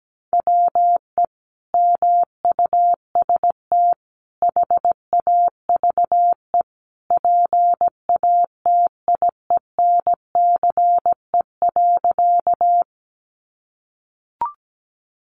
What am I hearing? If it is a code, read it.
Morse code, heard as a keyed tone, about 17 wpm, 700 Hz. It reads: WE MUST HAVE PATIENCE.
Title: Sentences from Top 1300 Words 17wpm